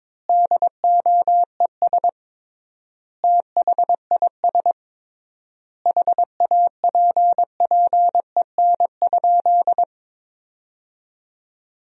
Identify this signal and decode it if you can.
Morse code, heard as a keyed tone, about 22 wpm, 700 Hz. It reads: DOES THIS HAPPEN?
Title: 3x Word-Spacing - Sentences from Top 400 Words - 22wpm